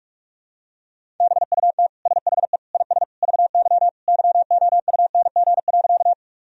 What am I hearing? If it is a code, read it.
Morse code, heard as a keyed tone, about 42 wpm, 700 Hz. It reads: BUT SHE IS VY YOUNG.
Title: Sentences from Top 400 Words and Abbreviations 45wpm